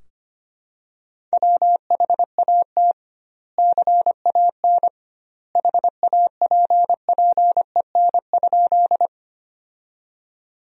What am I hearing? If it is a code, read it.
Morse code, heard as a keyed tone, about 25 wpm, 700 Hz. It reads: WHAT CAN HAPPEN?
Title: 2x Word-Spacing - Sentences from Top 400 Words - 25wpm